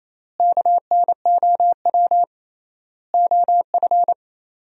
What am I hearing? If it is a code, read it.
Morse code, heard as a keyed tone, about 28 wpm, 700 Hz. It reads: KNOW OF
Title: 3x Word-Spacing - Sentences from Top 100 Words - 28wpm